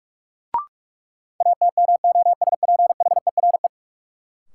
Morse code, heard as a keyed tone, about 46 wpm, 700 Hz. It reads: ATMOSPHERE
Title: Sentences from Top 1600 Words - New Word Drill Alt - 45wpm